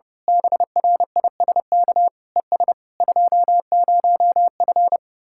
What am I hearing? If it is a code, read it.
Morse code, heard as a keyed tone, about 29 wpm, 700 Hz. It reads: BRISK ES 20F
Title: Speed-Racing Common QSO Phrases Encore - 30wpm